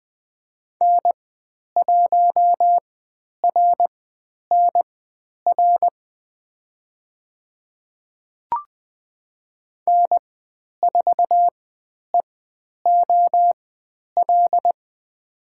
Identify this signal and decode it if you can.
Morse code, heard as a keyed tone, about 20 wpm, spaced as for 10 wpm, 700 Hz. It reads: N1RNR N4EOL
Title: 095 Lesson 16g - Letter c - Call Signs 20wpm